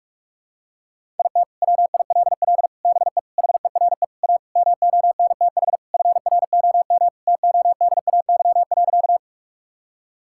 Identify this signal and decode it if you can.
Morse code, heard as a keyed tone, about 46 wpm, 700 Hz. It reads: IT WILL BE HERE A MONTH FROM TODAY.
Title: Sentences from Top 1000 Words - Review Part 5 - 45wpm